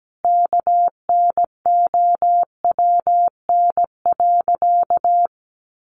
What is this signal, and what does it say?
Morse code, heard as a keyed tone, about 17 wpm, 700 Hz. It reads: KNOWN.
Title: Sentences from Top 1300 Words 17wpm